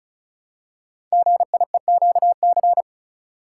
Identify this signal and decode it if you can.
Morse code, heard as a keyed tone, about 36 wpm, 700 Hz. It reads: GIEQC
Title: Code Groups - 5 Letter:Numbers - 35wpm